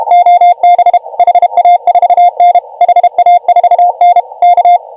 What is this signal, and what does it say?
Morse code, heard as a keyed tone, about 32 wpm, 720 Hz. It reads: OB HA4N HA4N K
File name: AUDIO2019 / HG1G CW